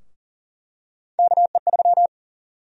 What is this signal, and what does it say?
Morse code, heard as a keyed tone, about 40 wpm, 700 Hz. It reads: KE3
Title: USA Call Sign Prefixes - 40wpm